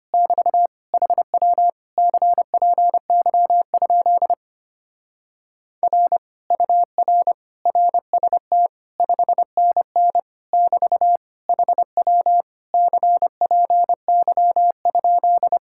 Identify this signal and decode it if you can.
Morse code, heard as a keyed tone, about 29 wpm, 700 Hz. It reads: <BT> HW CPY? R UR RST 5NN <BT> HW CPY?
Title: Speed-Racing Common QSO Phrases - 25wpm